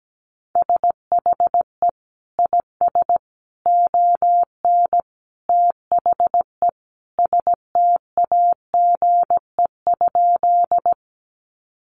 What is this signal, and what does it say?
Morse code, heard as a keyed tone, about 17 wpm, 700 Hz. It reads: SHE IS ON THE STAGE?
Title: Sentences from Top 1200 Words 17wpm